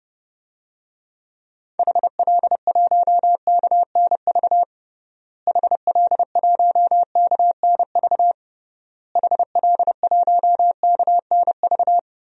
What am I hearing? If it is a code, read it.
Morse code, heard as a keyed tone, about 30 wpm, 700 Hz. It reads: HL1KNV HL1KNV HL1KNV
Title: DXCC-100 Call Signs - 6 Characters - Repeated 3x - 30wpm